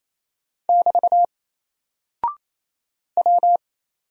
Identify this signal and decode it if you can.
Morse code, heard as a keyed tone, about 28 wpm, 700 Hz. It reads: <BT> W
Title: Single Character (Letter:Number:Punctuation:Prosign) Alt 28wpm